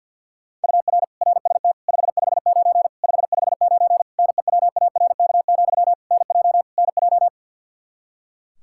Morse code, heard as a keyed tone, about 50 wpm, 700 Hz. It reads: UR RST 559 559 NEWARK, NJ NJ
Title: Practice QSO - Exchange 1 - Beginning - 50wpm